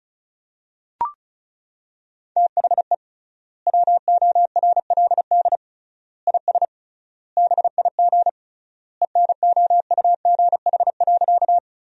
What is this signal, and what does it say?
Morse code, heard as a keyed tone, about 34 wpm, 700 Hz. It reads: THE WORLD IS BIG ENOUGH.
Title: 3x Word-Spacing - Sentences from Top 400 Words - 35wpm